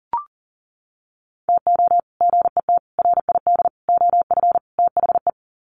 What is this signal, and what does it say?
Morse code, heard as a keyed tone, about 40 wpm, 700 Hz. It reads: TO GET RID OF THE
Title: Sets of 5 Words 40wpm